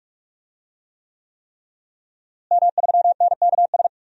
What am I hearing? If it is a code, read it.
Morse code, heard as a keyed tone, about 46 wpm, 700 Hz. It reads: M3NKS
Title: DXCC-100 Call Signs - 5 Characters - Repeated 3x - 45wpm